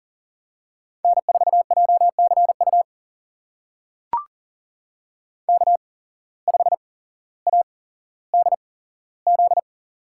Morse code, heard as a keyed tone, about 40 wpm, spaced as for 17 wpm, 700 Hz. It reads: N4JCU K5ADZ
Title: USA Call Signs - 5 Characters - 8x Character Spacing - 40wpm